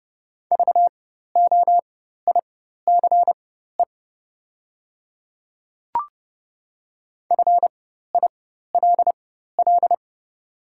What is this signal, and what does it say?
Morse code, heard as a keyed tone, about 30 wpm, spaced as for 14 wpm, 700 Hz. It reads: VOICE FILL
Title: 4x Character Spacing - Top 500 Words - 30wpm